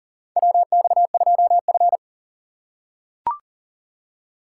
Morse code, heard as a keyed tone, about 40 wpm, 700 Hz. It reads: WX2F
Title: USA Call Signs - 4 Characters - 4x Character Spacing - 40wpm